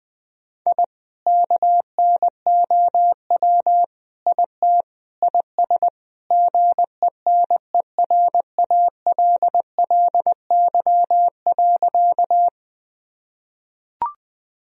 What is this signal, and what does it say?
Morse code, heard as a keyed tone, about 21 wpm, 700 Hz. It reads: I KNOW IT IS GENERALLY.
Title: Sentences from Top 1000 Words - Review Part 4 - 20wpm